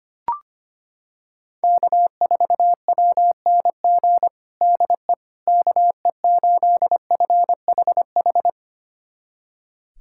Morse code, heard as a keyed tone, about 25 wpm, 700 Hz. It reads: K4WNG DE KE8FHH
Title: USA Callsign DE Callsign 25wpm